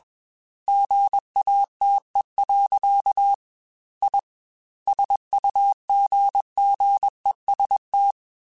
Morse code, heard as a keyed tone, about 21 wpm, 800 Hz. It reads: GATE. I SUGGEST
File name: Technical Subjects / Technical Subjects-mp3s / AND Gate @20